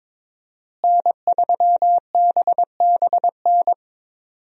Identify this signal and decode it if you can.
Morse code, heard as a keyed tone, about 22 wpm, 700 Hz. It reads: N3BBN
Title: USA Call Signs - 5 Characters - Repeated 2x - 22wpm